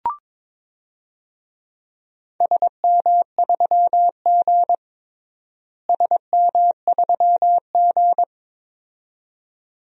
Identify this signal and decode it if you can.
Morse code, heard as a keyed tone, about 22 wpm, 700 Hz. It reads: SM3G SM3G
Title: DXCC-100 Call Signs - 4 Characters - Repeated 2x - 22wpm